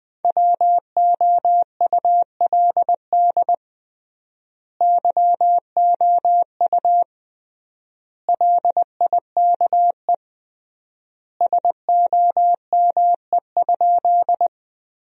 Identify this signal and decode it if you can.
Morse code, heard as a keyed tone, about 20 wpm, 700 Hz. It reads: WOULD YOU LIKE SOME?
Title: 3x Word-Spacing - Sentences from Top 100 Words - 20wpm